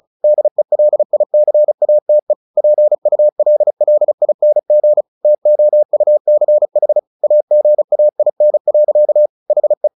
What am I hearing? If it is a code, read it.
Morse code, heard as a keyed tone, about 34 wpm, 600 Hz. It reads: DELICATE PULLING TOUCH AGAIN. HE
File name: the old man and the sea / the old man and the sea 1 35wpm